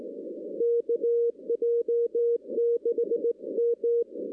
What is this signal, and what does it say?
Morse code, heard as a keyed tone, about 17 wpm, 470 Hz. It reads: KJ6M
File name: 2016 arrl dx cw / 2016-ARRL-DX-CW-K5TR- - 0-25